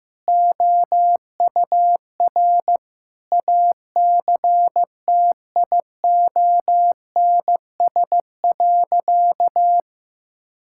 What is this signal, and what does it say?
Morse code, heard as a keyed tone, about 15 wpm, 700 Hz. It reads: OUR ACTIONS.
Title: Sentences from Top 1000 Words - Review Part 6 - 15wpm